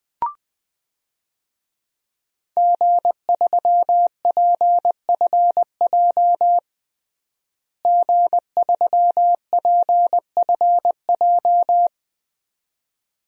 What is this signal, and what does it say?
Morse code, heard as a keyed tone, about 20 wpm, 700 Hz. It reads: G3PFJ G3PFJ
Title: DXCC-100 Call Signs - 5 Characters - Repeated 2x - 20wpm